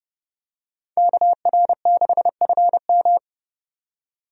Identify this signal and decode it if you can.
Morse code, heard as a keyed tone, about 32 wpm, 700 Hz. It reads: KR6FM
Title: USA Call Signs - 5 Characters - 30wpm